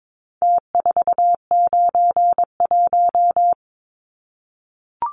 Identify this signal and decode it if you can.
Morse code, heard as a keyed tone, about 22 wpm, 700 Hz. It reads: T491
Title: Numbers - Contest Style - 4-Digit 22wpm